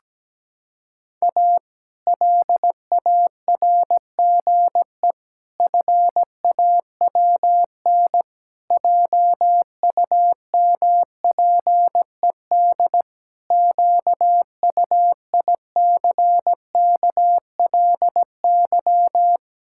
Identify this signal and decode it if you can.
Morse code, heard as a keyed tone, about 17 wpm, 700 Hz. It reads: A LARGE FAWN JUMPED QUICKLY
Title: Pangram Challenge v2 - 17wpm